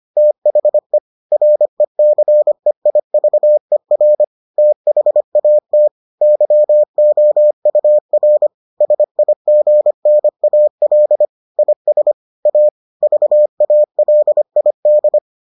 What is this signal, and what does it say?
Morse code, heard as a keyed tone, about 25 wpm, 600 Hz. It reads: THE RECEIVER THAT YOUR SIGNAL IS A VALID